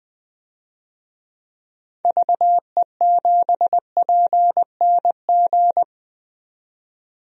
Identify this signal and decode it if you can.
Morse code, heard as a keyed tone, about 20 wpm, 700 Hz. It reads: VE7PNG
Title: DXCC-100 Call Signs - 6 Characters - 20wpm